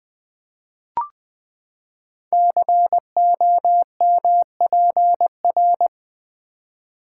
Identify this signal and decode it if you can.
Morse code, heard as a keyed tone, about 20 wpm, 700 Hz. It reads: COMPR
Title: Build A Word - 5 Letters At A Time - 10-Letter Words - 20wpm